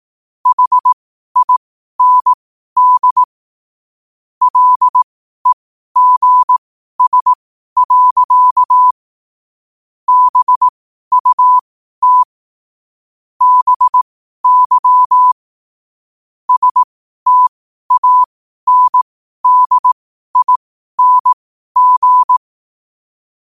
Morse code, heard as a keyed tone, about 18 wpm, spaced as for 12 wpm, 1000 Hz. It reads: HIND LEGS. BUT BY STANDING
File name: qotdfiles / 12wpm